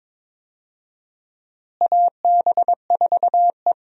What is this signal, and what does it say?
Morse code, heard as a keyed tone, about 22 wpm, 700 Hz. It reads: AB4E
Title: USA Call Signs - 4 Characters - Repeated 3x - 22wpm